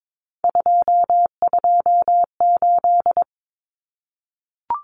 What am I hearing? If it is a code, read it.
Morse code, heard as a keyed tone, about 22 wpm, 700 Hz. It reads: 228
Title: Numbers - 3-Digit 22wpm